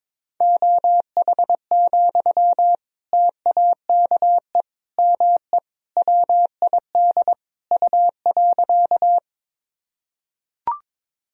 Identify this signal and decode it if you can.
Morse code, heard as a keyed tone, about 21 wpm, 700 Hz. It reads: OH, TAKE ME WID U.
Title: Sentences from Top 500 Words and Abbreviations 22wpm